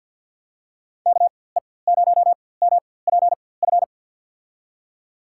Morse code, heard as a keyed tone, about 53 wpm, spaced as for 24 wpm, 700 Hz. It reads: KE0MPF
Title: USA Call Signs - 4x Character Spacing - 50wpm